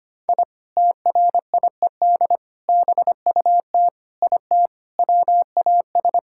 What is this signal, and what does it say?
Morse code, heard as a keyed tone, about 25 wpm, 700 Hz. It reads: I TRIED BUT IT WAS
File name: practice / 25wpm / Sentences with 3-Word Phrases - 25wpm